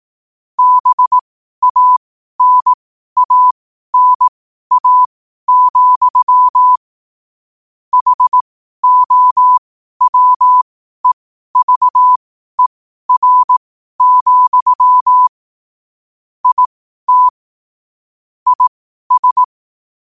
Morse code, heard as a keyed tone, about 18 wpm, spaced as for 12 wpm, 1000 Hz. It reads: BANANA, HOWEVER, IT IS
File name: qotdfiles / 12wpm